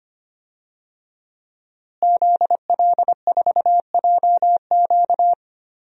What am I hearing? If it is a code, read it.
Morse code, heard as a keyed tone, about 25 wpm, 700 Hz. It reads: ZL4JQ
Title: DXCC-100 Call Signs - 5 Characters - Repeated 3x - 25wpm